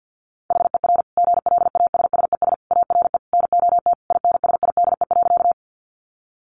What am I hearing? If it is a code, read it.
Morse code, heard as a keyed tone, about 50 wpm, 700 Hz. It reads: HER GLASSES ARE NOT INSIDE.
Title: Additional Easy Phrases 50wpm